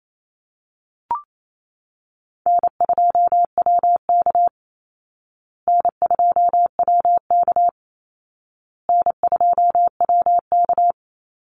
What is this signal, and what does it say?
Morse code, heard as a keyed tone, about 28 wpm, 700 Hz. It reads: N2WK N2WK N2WK
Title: Easy Call Signs Repeated 3x - 28wpm